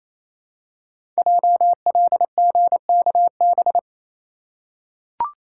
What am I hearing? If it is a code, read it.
Morse code, heard as a keyed tone, about 29 wpm, 700 Hz. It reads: JLGKB
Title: Code Groups - 5 Letters - 28wpm